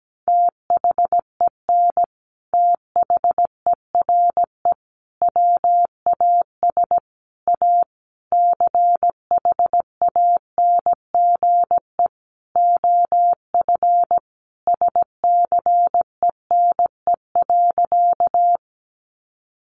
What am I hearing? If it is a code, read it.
Morse code, heard as a keyed tone, about 17 wpm, 700 Hz. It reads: THEN THERE WAS A CHANGE OF SCENE.
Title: Sentences from Top 1100 Words 17wpm